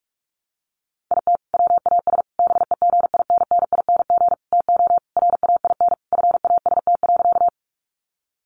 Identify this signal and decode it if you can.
Morse code, heard as a keyed tone, about 44 wpm, 700 Hz. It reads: IT WAS BEGINNING TO RAIN FAST.
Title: Sentences from Top 800 Words 45wpm